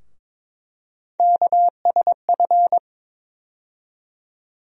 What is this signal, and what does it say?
Morse code, heard as a keyed tone, about 22 wpm, 700 Hz. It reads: KSF
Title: USA Call Sign Postfixes - 3 Letters 22wpm